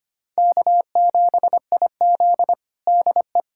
Morse code, heard as a keyed tone, about 25 wpm, 700 Hz. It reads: K7IZ DE
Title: USA Callsign DE Callsign - 4 Characters - 25wpm